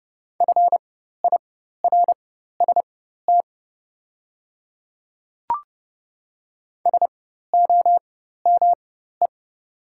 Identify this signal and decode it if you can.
Morse code, heard as a keyed tone, about 30 wpm, spaced as for 14 wpm, 700 Hz. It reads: FIRST SOME
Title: 4x Character Spacing - Top 100 Words - 30wpm